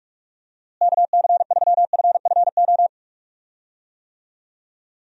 Morse code, heard as a keyed tone, about 46 wpm, 700 Hz. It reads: KC3FFO
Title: USA Call Signs - 6 Characters - Repeated 3x - 45wpm